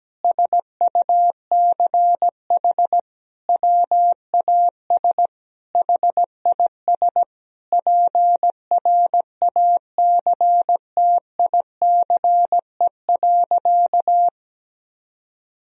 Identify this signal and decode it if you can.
Morse code, heard as a keyed tone, about 17 wpm, 700 Hz. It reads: SUCH WAS HIS PRACTICE.
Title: Sentences from Top 1000 Words - Review Part 4 - 17wpm